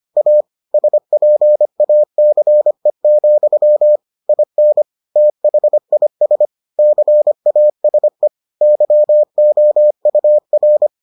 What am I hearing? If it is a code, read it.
Morse code, heard as a keyed tone, about 25 wpm, 600 Hz. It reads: A SPACE, IN THIS CASE YOUR